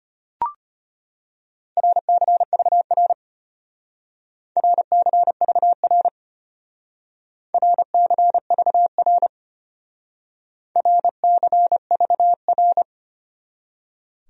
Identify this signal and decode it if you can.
Morse code, heard as a keyed tone, about 33 wpm, 700 Hz. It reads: RCVR RCVR RCVR RCVR